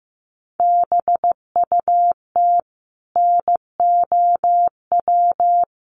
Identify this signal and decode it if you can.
Morse code, heard as a keyed tone, about 15 wpm, 700 Hz. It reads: BUT NOW
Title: Sentences from Top 1000 Words 15wpm